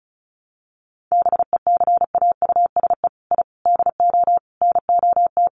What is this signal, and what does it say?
Morse code, heard as a keyed tone, about 34 wpm, 700 Hz. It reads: BECAUSE I DO NOT
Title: Sets of 5 Words Encore2 35wpm